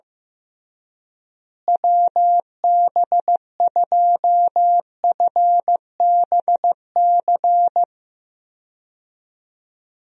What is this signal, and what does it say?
Morse code, heard as a keyed tone, about 15 wpm, 700 Hz. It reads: WB2FBC
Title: USA Call Signs - 6 Characters - Repeated 3x - 15wpm